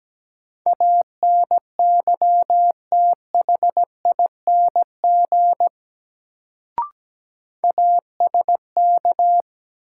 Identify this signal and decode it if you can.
Morse code, heard as a keyed tone, about 17 wpm, 700 Hz. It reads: ANYTHING ASK
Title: Sets of 4 Words - New Word Drill - 17wpm